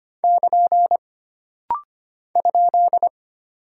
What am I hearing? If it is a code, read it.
Morse code, heard as a keyed tone, about 25 wpm, 700 Hz. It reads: <KN> ?
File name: Single Punctuation-Prosign - Rapid-Fire - 25wpm